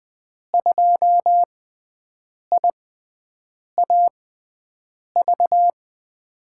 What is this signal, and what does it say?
Morse code, heard as a keyed tone, about 20 wpm, spaced as for 7 wpm, 700 Hz. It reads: 2IAV
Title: USA Call Signs - 5 Characters - 6x Character Spacing - 20wpm